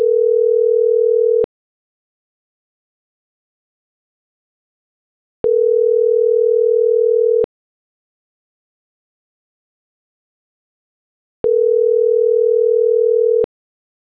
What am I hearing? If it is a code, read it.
Ringback tone (North America), 440 + 480 Hz.